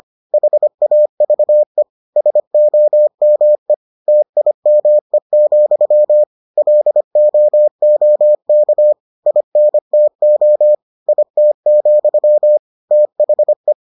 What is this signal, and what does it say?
Morse code, heard as a keyed tone, about 25 wpm, 600 Hz. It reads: HAVE SOME TIME, LOOK INTO IT, THE